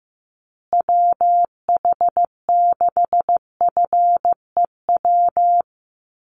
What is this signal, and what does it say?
Morse code, heard as a keyed tone, about 15 wpm, 700 Hz. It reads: WH6FEW
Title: USA Call Signs Repeated 3x 15wpm